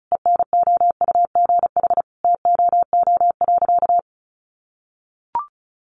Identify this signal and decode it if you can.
Morse code, heard as a keyed tone, about 34 wpm, 700 Hz. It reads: ENOUGH TOO.
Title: Additional Easy Phrases 35wpm